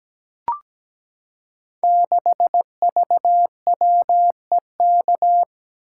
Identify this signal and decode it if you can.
Morse code, heard as a keyed tone, about 17 wpm, 700 Hz. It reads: 6VWEK
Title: Code Groups - 5 Letter:Numbers - 17wpm